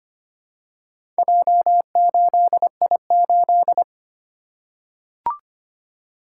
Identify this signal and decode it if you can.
Morse code, heard as a keyed tone, about 25 wpm, 700 Hz. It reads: J8I8
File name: practice / 25wpm / Four Letter-Number Combinations - 25wpm